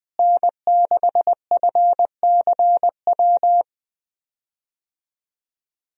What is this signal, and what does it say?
Morse code, heard as a keyed tone, about 21 wpm, 700 Hz. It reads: N6FCW
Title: USA Call Signs - 5 Characters - Repeated 3x - 20wpm